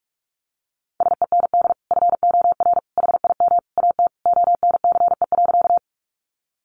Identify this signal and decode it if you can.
Morse code, heard as a keyed tone, about 44 wpm, 700 Hz. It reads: SEND FOR HIM AT ONCE.
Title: Sentences from Top 1500 Words 45wpm